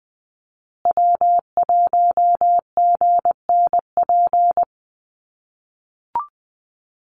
Morse code, heard as a keyed tone, about 20 wpm, 700 Hz. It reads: W1GNP
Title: USA Call Signs Repeated 3x 20wpm